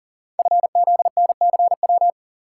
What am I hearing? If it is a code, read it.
Morse code, heard as a keyed tone, about 38 wpm, 690 Hz. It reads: FZNCW
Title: Code Groups - 5 Letters - 40wpm